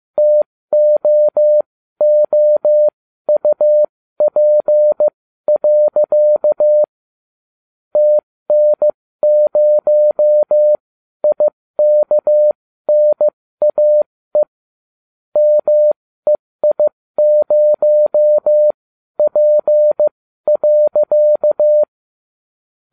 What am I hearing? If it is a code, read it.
Morse code, heard as a keyed tone, about 15 wpm, spaced as for 13 wpm, 600 Hz. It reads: TOOUP. TN0IKNAE MEI0P.